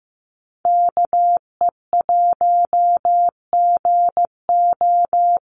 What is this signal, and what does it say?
Morse code, heard as a keyed tone, about 15 wpm, 700 Hz. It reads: KE1GO
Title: USA Call Signs 15wpm